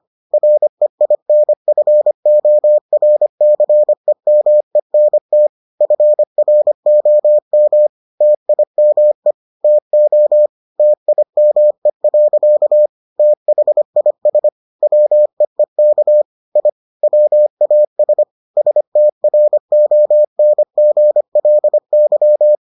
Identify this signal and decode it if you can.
Morse code, heard as a keyed tone, about 25 wpm, 600 Hz. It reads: REINFORCEMENT FROM TIME TO TIME. THIS WEEK I WAS STRONGLY